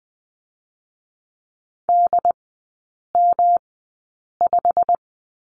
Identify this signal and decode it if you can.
Morse code, heard as a keyed tone, about 20 wpm, spaced as for 8 wpm, 700 Hz. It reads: DM5
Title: 121 Lesson 20e - Number 3 - Three Characters 20wpm